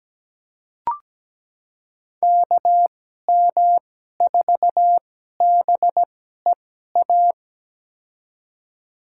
Morse code, heard as a keyed tone, about 17 wpm, spaced as for 12 wpm, 700 Hz. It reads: KM4BEA
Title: USA Call Signs - 2x Character Spacing - 17wpm